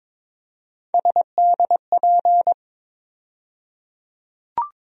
Morse code, heard as a keyed tone, about 22 wpm, 700 Hz. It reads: SDP
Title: USA Call Sign Postfixes - 3 Letters 22wpm